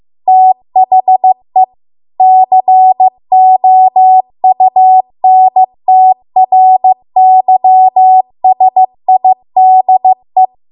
Morse code, heard as a keyed tone, about 15 wpm, 750 Hz. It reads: THE COUNTRYSIDE
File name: Archive / 15 WPM / 15WPM